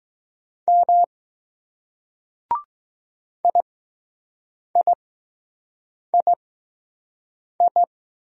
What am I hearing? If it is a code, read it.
Morse code, heard as a keyed tone, about 20 wpm, spaced as for 6 wpm, 700 Hz. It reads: M IIII